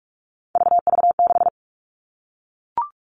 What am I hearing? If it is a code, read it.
Morse code, heard as a keyed tone, about 44 wpm, 700 Hz. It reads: VV6
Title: All Cut Numbers - 3-Digit 45wpm